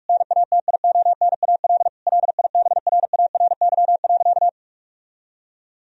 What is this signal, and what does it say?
Morse code, heard as a keyed tone, about 41 wpm, 700 Hz. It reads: NATIONAL LIBRARY.
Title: Sentences from Top 1000 Words - Review Part 4 - 45wpm